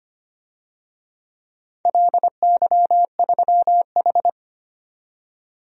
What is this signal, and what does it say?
Morse code, heard as a keyed tone, about 25 wpm, 700 Hz. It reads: LY3H
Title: DXCC-100 Call Signs - 4 Characters - 25wpm